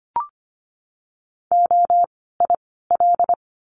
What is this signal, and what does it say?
Morse code, heard as a keyed tone, about 25 wpm, 700 Hz. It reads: O I L
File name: course / 25:10wpm / 094 Lesson 16f - Letter c - Words 25wpm